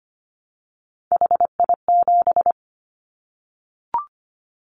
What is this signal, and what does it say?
Morse code, heard as a keyed tone, about 25 wpm, 700 Hz. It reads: HI7
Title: Callsign Prefixes for Top 100 DXCC - 25wpm